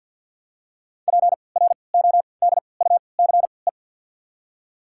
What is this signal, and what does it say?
Morse code, heard as a keyed tone, about 53 wpm, spaced as for 29 wpm, 690 Hz. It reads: PRODUCE
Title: 3x Character Spacing - Top 500 Words - 50wpm